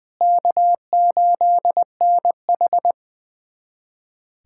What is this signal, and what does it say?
Morse code, heard as a keyed tone, about 20 wpm, 700 Hz. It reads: K8NH
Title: USA Callsign DE Callsign - 4 Characters - 20wpm